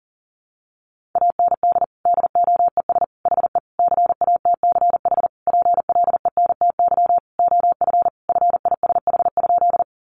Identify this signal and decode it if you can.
Morse code, heard as a keyed tone, about 40 wpm, 700 Hz. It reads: AND DOES HE CATCH PLENTY OF FISH?
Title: Sentences from Top 1500 Words 40wpm